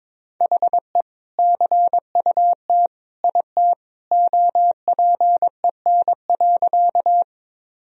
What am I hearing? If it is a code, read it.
Morse code, heard as a keyed tone, about 22 wpm, 700 Hz. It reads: HE CUT IT OPEN.
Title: Sentences from Top 1000 Words - Review Part 1 - 22wpm